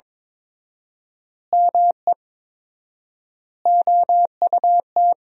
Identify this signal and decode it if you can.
Morse code, heard as a keyed tone, about 22 wpm, 700 Hz. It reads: ME OUT
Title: 4x Word-Spacing - Sentences from Top 400 Words - 22wpm